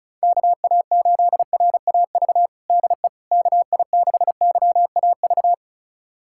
Morse code, heard as a keyed tone, about 34 wpm, 700 Hz. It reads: KA8RAV DE KI6YAV
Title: USA Callsign DE Callsign - 6 Characters - 35wpm